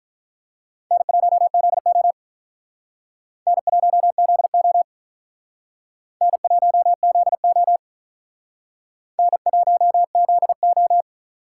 Morse code, heard as a keyed tone, about 43 wpm, 700 Hz. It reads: N1ZO N1ZO N1ZO N1ZO
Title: Speed Racing - USA 1x2 Call Signs 35wpm